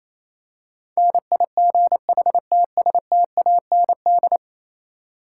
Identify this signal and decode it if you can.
Morse code, heard as a keyed tone, about 27 wpm, 700 Hz. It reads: NIGHTSTAND
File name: Easy Words - Speed-Racing with Hints - 22wpm